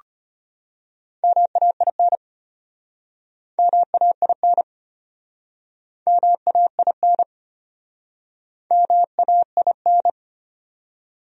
Morse code, heard as a keyed tone, about 33 wpm, 700 Hz. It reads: MAIN MAIN MAIN MAIN